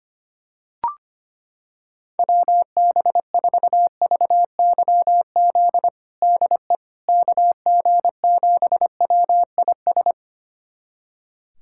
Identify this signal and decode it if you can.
Morse code, heard as a keyed tone, about 25 wpm, 700 Hz. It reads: WB4VYZ DE KG7WIS
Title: USA Callsign DE Callsign 25wpm